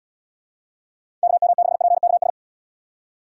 Morse code, heard as a keyed tone, about 78 wpm, 700 Hz. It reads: KM6PKS